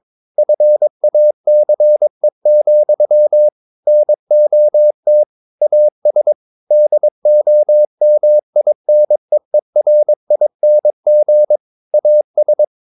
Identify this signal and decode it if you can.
Morse code, heard as a keyed tone, about 22 wpm, 600 Hz. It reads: FACE, NOT AS DOMINEERING AS
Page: Romanen Never af Ken Follet på engelsk.